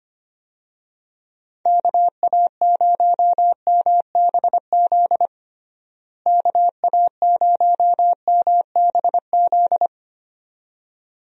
Morse code, heard as a keyed tone, about 25 wpm, 700 Hz. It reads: KA0MBZ KA0MBZ
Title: USA Call Signs - 6 Characters - Repeated 2x - 25wpm